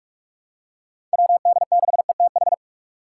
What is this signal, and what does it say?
Morse code, heard as a keyed tone, about 46 wpm, 700 Hz. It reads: WD6ETH
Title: USA Call Signs - 6 Characters - 6x Character Spacing - 45wpm